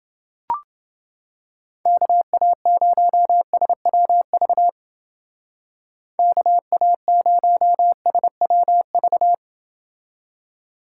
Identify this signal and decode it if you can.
Morse code, heard as a keyed tone, about 28 wpm, 700 Hz. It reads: KA0SWV KA0SWV